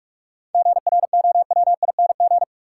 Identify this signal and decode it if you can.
Morse code, heard as a keyed tone, about 46 wpm, 700 Hz. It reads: GROWING
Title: Sets of 5 Words Encore - New Word Drill - 45wpm